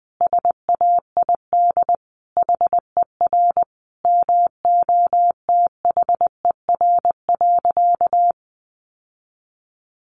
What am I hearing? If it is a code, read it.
Morse code, heard as a keyed tone, about 20 wpm, 700 Hz. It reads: SAID HER MOTHER.
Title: Sentences from Top 700 Words 20wpm